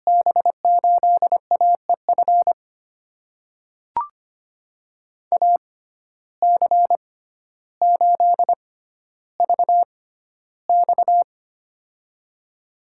Morse code, heard as a keyed tone, about 25 wpm, spaced as for 13 wpm, 700 Hz. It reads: B8AEF AC8VX
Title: USA Call Signs - 6x Character Spacing - 25wpm